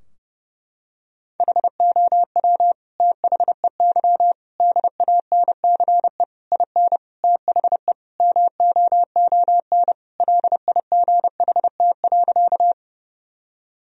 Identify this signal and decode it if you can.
Morse code, heard as a keyed tone, about 30 wpm, 700 Hz. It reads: HOW THEY DANCE IN THE MOON LIGHT.
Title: Sentences from Top 1000 Words - Review Part 4 - 30wpm